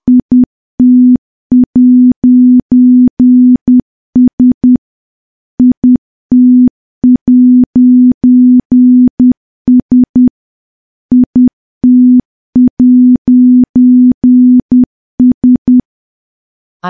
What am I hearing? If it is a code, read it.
Morse code, heard as a keyed tone, about 10 wpm, 250 Hz. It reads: IT'S IT'S IT'S
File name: morse / morse-training-10WPM